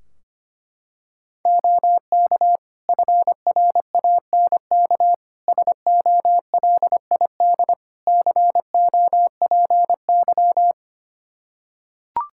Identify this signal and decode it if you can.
Morse code, heard as a keyed tone, about 25 wpm, 700 Hz. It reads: OK FRANK SOLID COPY
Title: Practice QSO - Exchange 3 - More Info - 25wpm